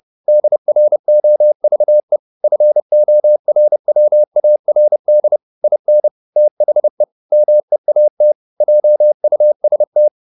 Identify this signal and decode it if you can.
Morse code, heard as a keyed tone, about 30 wpm, 600 Hz. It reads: DROVE FORWARD IN THE MEAT JUST